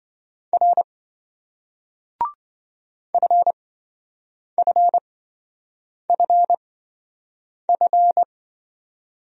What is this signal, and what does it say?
Morse code, heard as a keyed tone, about 27 wpm, spaced as for 7 wpm, 700 Hz. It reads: R FFFF